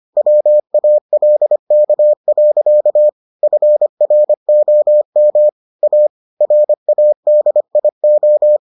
Morse code, heard as a keyed tone, about 25 wpm, 600 Hz. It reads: WALK. FROM A RADIO